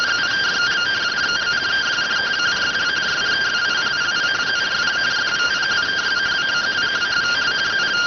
Two-tone teleprinter signal, about 170 Hz shift.